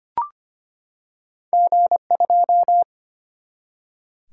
Morse code, heard as a keyed tone, about 25 wpm, 700 Hz. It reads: G2
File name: Two Letter-Number Combinations - 25wpm